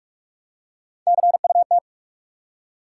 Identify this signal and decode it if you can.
Morse code, heard as a keyed tone, about 46 wpm, 700 Hz. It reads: CUT
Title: Three Letter Words - 45wpm